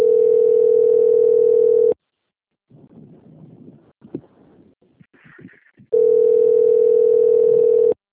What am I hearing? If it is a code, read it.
Ringback tone (North America), 440 + 480 Hz.